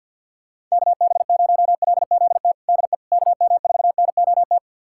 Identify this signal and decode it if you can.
Morse code, heard as a keyed tone, about 52 wpm, 700 Hz. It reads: KB0LZT DE KM4NOT
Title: USA Callsign DE Callsign - 6 Characters - 50wpm